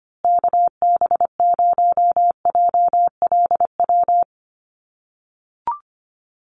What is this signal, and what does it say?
Morse code, heard as a keyed tone, about 25 wpm, 700 Hz. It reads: KB0JLW
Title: USA Call Signs Repeated 2x 25wpm